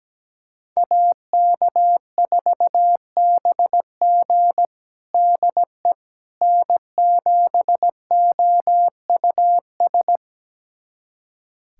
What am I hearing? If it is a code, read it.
Morse code, heard as a keyed tone, about 17 wpm, 700 Hz. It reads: AK4BG DE N7OUS
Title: USA Callsign DE Callsign - 5 Characters - 17wpm